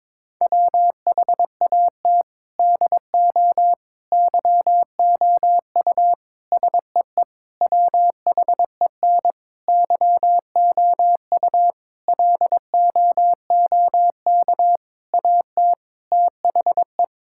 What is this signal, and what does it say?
Morse code, heard as a keyed tone, about 21 wpm, 700 Hz. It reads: WHAT DO YOU SEE WHEN YOU LOOK AT THE
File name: practice / 22wpm / Sentences with 4-Word Phrases - 22wpm